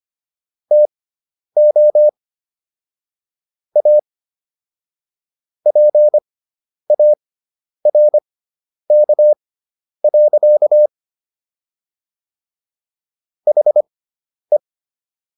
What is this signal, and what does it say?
Morse code, heard as a keyed tone, about 25 wpm, spaced as for 10 wpm, 600 Hz. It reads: TO A PARK. HE